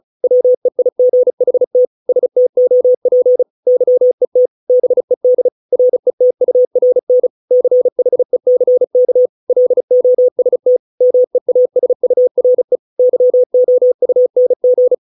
Morse code, heard as a keyed tone, about 34 wpm, 500 Hz. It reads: WEIGHT STOP YET BED RETURN CHECK LOST MEASURE YOUNG